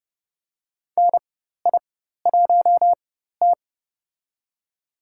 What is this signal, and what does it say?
Morse code, heard as a keyed tone, about 30 wpm, spaced as for 14 wpm, 700 Hz. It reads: NI1T
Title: USA Call Signs - 4x Character Spacing - 30wpm